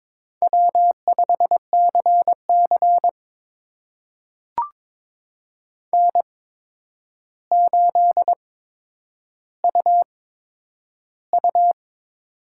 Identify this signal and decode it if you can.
Morse code, heard as a keyed tone, about 22 wpm, spaced as for 9 wpm, 700 Hz. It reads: W5CC N8UU
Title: USA Call Signs - 8x Character Spacing - 22wpm